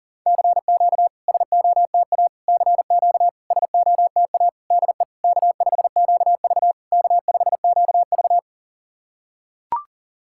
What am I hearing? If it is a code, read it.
Morse code, heard as a keyed tone, about 39 wpm, 690 Hz. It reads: CQ SOTA CQ SOTA DE K5QV K5QV
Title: SOTA QSO Practice 40wpm